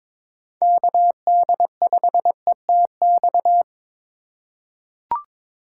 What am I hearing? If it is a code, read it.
Morse code, heard as a keyed tone, about 21 wpm, 700 Hz. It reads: KD5ETX
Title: USA Call Signs - 2x Character Spacing - 22wpm